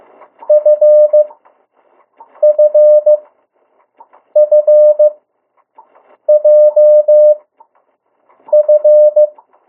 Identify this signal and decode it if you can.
Morse code, heard as a keyed tone, about 15 wpm, spaced as for 6 wpm, 600 Hz. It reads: FFFJF
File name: morse mp3 / 02 urok 1